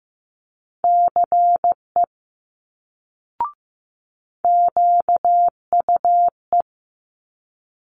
Morse code, heard as a keyed tone, about 15 wpm, 700 Hz. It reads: CE QUE
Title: Multi-letter Phonemes Quick Fire 15wpm